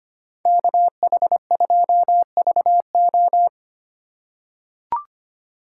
Morse code, heard as a keyed tone, about 25 wpm, 700 Hz. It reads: KH2VO
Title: USA Call Signs - 5 Characters - 4x Character Spacing - 25wpm